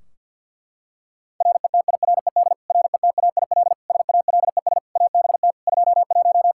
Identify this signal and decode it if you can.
Morse code, heard as a keyed tone, about 47 wpm, 700 Hz. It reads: RETIRED RETAIL SALES ABT 21
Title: Practice QSO - Exchange 3 - More Info - 50wpm